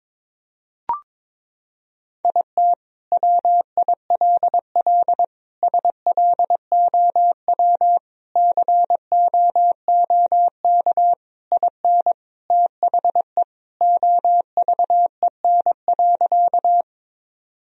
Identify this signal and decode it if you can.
Morse code, heard as a keyed tone, about 22 wpm, 700 Hz. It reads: IT WILL SLOW COOK IN THE OVEN.
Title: Sentences from Top 1000 Words - Review Part 4 - 22wpm